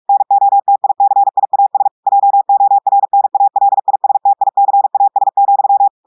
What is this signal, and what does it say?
Morse code, heard as a keyed tone, about 46 wpm, 800 Hz. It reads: NOTICIAS JORNALISTICAS,
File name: audios / portugais / portugais texte en clair 2 vitesse 45 mots minutes